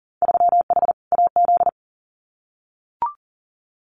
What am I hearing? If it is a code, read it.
Morse code, heard as a keyed tone, about 40 wpm, 700 Hz. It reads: 3H AZ
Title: Winter Field Day Exchanges Encore 40wpm